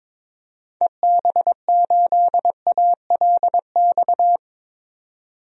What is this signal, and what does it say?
Morse code, heard as a keyed tone, about 22 wpm, 700 Hz. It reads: EB8ALX
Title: DXCC-100 Call Signs - 6 Characters - 22wpm